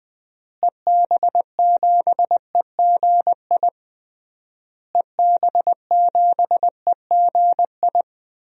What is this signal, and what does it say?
Morse code, heard as a keyed tone, about 19 wpm, 700 Hz. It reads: EB7EGI EB7EGI
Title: DXCC-100 Call Signs - 6 Characters - Repeated 3x - 20wpm